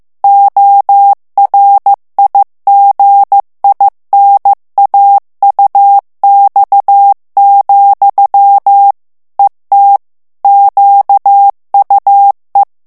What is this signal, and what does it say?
Morse code, heard as a keyed tone, about 15 wpm, 800 Hz. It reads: ORIGINAUX, ET QUE
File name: morse mp3 / FRANCAIS TEXTE CLAIR 2 VITESSE 14 MOTS MINUTES